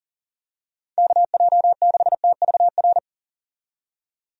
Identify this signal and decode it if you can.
Morse code, heard as a keyed tone, about 40 wpm, 700 Hz. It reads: KJ6TVR
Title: USA Call Signs - 6 Characters - 40wpm